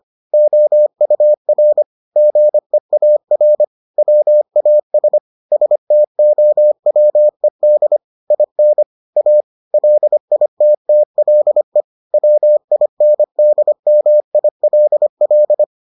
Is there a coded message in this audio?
Yes: Morse code from 0.3 to 15.6 s, heard as a keyed tone, about 25 wpm, 600 Hz. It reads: OUR GEAR WAS STOWED IN A LITTLE WINDMILL